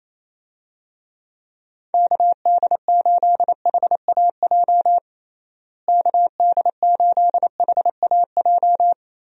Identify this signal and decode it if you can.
Morse code, heard as a keyed tone, about 28 wpm, 700 Hz. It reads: KD8HAJ KD8HAJ
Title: USA Call Signs - 6 Characters - Repeated 3x - 28wpm